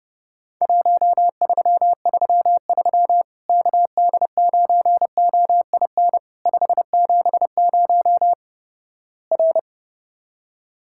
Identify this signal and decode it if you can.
Morse code, heard as a keyed tone, about 30 wpm, 700 Hz. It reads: 1333 KD9OIN 570 R
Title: Worked All Europe DX Contest - QTC Exchange - 30wpm